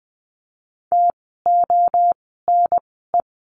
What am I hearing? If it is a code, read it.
Morse code, heard as a keyed tone, about 20 wpm, spaced as for 14 wpm, 700 Hz. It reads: TONE
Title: 068 Lesson 12f - Letter d - Words 20wpm